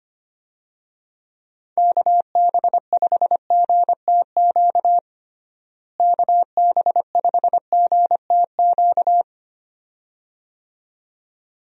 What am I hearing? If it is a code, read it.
Morse code, heard as a keyed tone, about 25 wpm, 700 Hz. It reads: KB5GTQ KB5GTQ
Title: USA Call Signs - 6 Characters - Repeated 2x - 25wpm